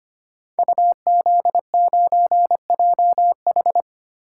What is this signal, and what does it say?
Morse code, heard as a keyed tone, about 25 wpm, 700 Hz. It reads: UZ9JH
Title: Code Groups - 5 Letter:Numbers - 25wpm